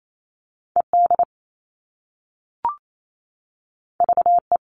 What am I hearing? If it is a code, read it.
Morse code, heard as a keyed tone, about 28 wpm, 700 Hz. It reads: ED VE
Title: All Cut Numbers - 2-Digit 28wpm